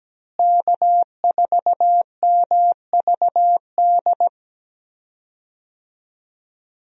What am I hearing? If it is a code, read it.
Morse code, heard as a keyed tone, about 17 wpm, 700 Hz. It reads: K4MVD
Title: USA Call Signs - 5 Characters - Repeated 2x - 17wpm